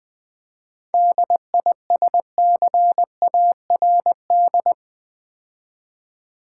Morse code, heard as a keyed tone, about 20 wpm, 700 Hz. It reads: DISCARD
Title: 7 Letter Words - First 3 Letters Given - 20wpm